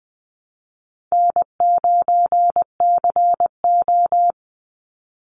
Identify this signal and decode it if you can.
Morse code, heard as a keyed tone, about 20 wpm, 700 Hz. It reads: N9CO
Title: Call Signs - 20wpm